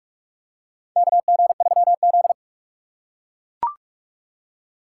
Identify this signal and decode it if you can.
Morse code, heard as a keyed tone, about 44 wpm, 690 Hz. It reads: KG3Z
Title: USA Call Signs - 4 Characters - 2x Character Spacing - 45wpm